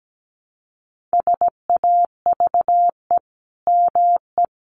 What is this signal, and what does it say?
Morse code, heard as a keyed tone, about 17 wpm, 700 Hz. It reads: SAVE ME
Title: Sentences from Top 1100 Words 17wpm